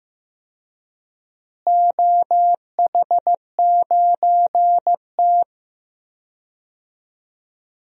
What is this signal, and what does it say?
Morse code, heard as a keyed tone, about 15 wpm, 700 Hz. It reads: OH9T
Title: DXCC-100 Call Signs - 4 Characters - Repeated 2x - 15wpm